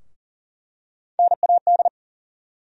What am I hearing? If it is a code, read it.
Morse code, heard as a keyed tone, about 40 wpm, 700 Hz. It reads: NAD
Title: USA Call Sign Postfixes - 3 Letters 40wpm